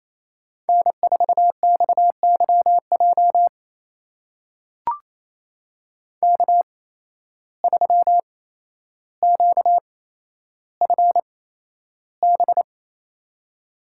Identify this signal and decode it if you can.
Morse code, heard as a keyed tone, about 27 wpm, spaced as for 12 wpm, 700 Hz. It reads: N4XYJ K3QFB
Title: USA Call Signs - 5 Characters - 8x Character Spacing - 28wpm